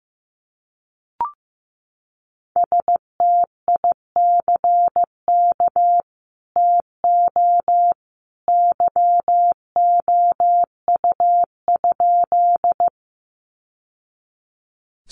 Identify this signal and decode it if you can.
Morse code, heard as a keyed tone, about 15 wpm, 700 Hz. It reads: STICK TO YOU?
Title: Sentences from Top 1200 Words 15wpm